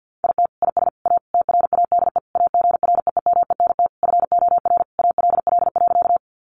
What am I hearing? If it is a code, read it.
Morse code, heard as a keyed tone, about 50 wpm, 700 Hz. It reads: IT IS A TRADE AGREEMENT FOR ALL.
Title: Sentences from Top 1500 Words 50wpm